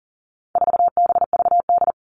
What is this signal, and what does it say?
Morse code, heard as a keyed tone, about 40 wpm, 700 Hz. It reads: 4BVD
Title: All Cut Numbers - 4-Digit 40wpm